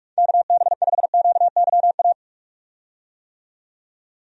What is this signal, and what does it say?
Morse code, heard as a keyed tone, about 41 wpm, 700 Hz. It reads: KB5QYA
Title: USA Call Signs - 6 Characters - 45wpm